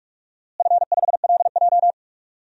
Morse code, heard as a keyed tone, about 42 wpm, 700 Hz. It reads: F5LJ
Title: DXCC-100 Call Signs - 4 Characters - Repeated 2x - 45wpm